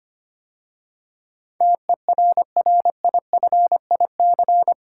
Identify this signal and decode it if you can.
Morse code, heard as a keyed tone, about 25 wpm, 700 Hz. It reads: TERRIFIC
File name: 8 Letter Words - First 3 Letters Given - 25wpm